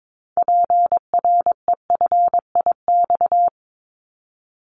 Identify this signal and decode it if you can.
Morse code, heard as a keyed tone, about 22 wpm, 700 Hz. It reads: PREFIX
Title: Callsign Prefixes for Top 100 DXCC - 22wpm